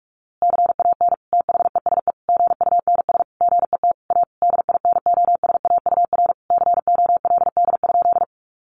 Morse code, heard as a keyed tone, about 44 wpm, 700 Hz. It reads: CAN THESE GUNS GET A DINOSAUR COLD?
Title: A Sound Of Thunder 45wpm